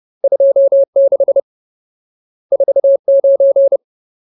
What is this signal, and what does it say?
Morse code, heard as a keyed tone, about 30 wpm, 550 Hz. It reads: 26 49
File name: Ultimate Word List / 30WPM Ultimate